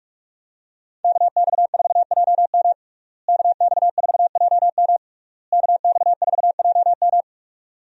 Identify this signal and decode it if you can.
Morse code, heard as a keyed tone, about 46 wpm, 700 Hz. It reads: KX4JM KX4JM KX4JM
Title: USA Call Signs - 5 Characters - Repeated 3x - 45wpm